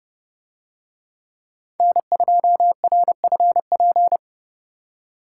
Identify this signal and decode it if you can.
Morse code, heard as a keyed tone, about 28 wpm, 700 Hz. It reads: N2RFP